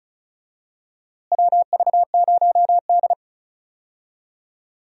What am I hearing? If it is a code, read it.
Morse code, heard as a keyed tone, about 36 wpm, 700 Hz. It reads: WV0D
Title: USA Call Signs - 4 Characters - Repeated 2x - 35wpm